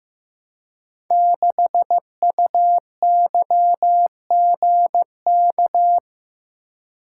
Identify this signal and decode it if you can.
Morse code, heard as a keyed tone, about 15 wpm, 700 Hz. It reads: 6UYGK
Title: Code Groups - 5 Letter:Numbers - 15wpm